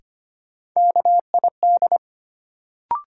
Morse code, heard as a keyed tone, about 25 wpm, 700 Hz. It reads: KID
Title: Sets of 5 Words - New Word Drill - 25wpm